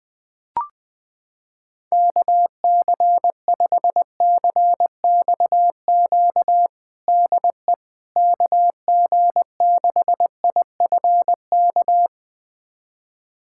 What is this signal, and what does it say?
Morse code, heard as a keyed tone, about 20 wpm, 700 Hz. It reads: KC5CXQ DE KG6IFK
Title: USA Callsign DE Callsign 20wpm